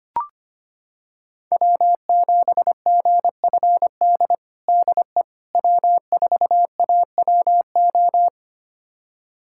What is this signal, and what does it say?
Morse code, heard as a keyed tone, about 25 wpm, 700 Hz. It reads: W7GFD DE W4AWO
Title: USA Callsign DE Callsign - 5 Characters - 25wpm